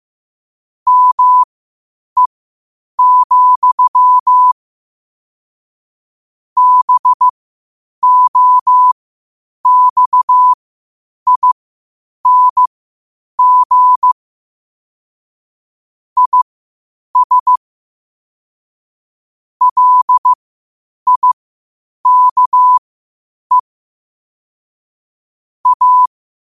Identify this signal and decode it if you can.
Morse code, heard as a keyed tone, about 15 wpm, spaced as for 8 wpm, 1000 Hz. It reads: ME, BOXING IS LIKE A
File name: qotdfiles / 8wpm